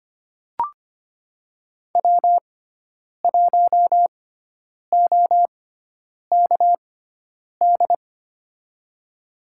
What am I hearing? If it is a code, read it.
Morse code, heard as a keyed tone, about 25 wpm, spaced as for 9 wpm, 700 Hz. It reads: W1OKD
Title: USA Call Signs - 5 Characters - 6x Character Spacing - 25wpm